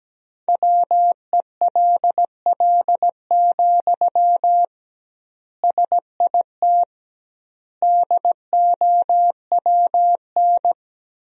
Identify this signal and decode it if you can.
Morse code, heard as a keyed tone, about 17 wpm, 700 Hz. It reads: WELL, SIT DOWN
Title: 2x Word-Spacing - Sentences from Top 500 Words - 17wpm